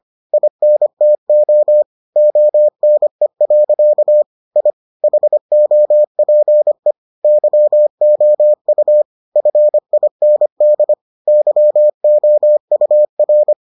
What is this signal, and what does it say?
Morse code, heard as a keyed tone, about 25 wpm, 600 Hz. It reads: INTO ONE. I HOPE YOU FIND YOUR